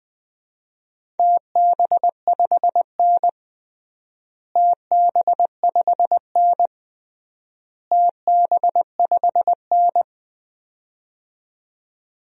Morse code, heard as a keyed tone, about 20 wpm, 700 Hz. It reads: TB5N TB5N TB5N
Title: DXCC-100 Call Signs - 4 Characters - Repeated 3x - 20wpm